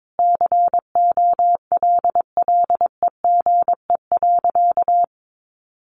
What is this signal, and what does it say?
Morse code, heard as a keyed tone, about 22 wpm, 700 Hz. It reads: COLLEGE.
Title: Sentences from Top 1300 Words 22wpm